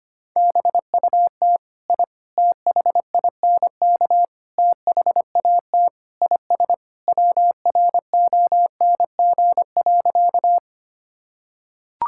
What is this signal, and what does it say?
Morse code, heard as a keyed tone, about 25 wpm, 700 Hz. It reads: BUT I THINK THAT IS WRONG.
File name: practice / 25wpm / Sentences with 4-Word Phrases - 25wpm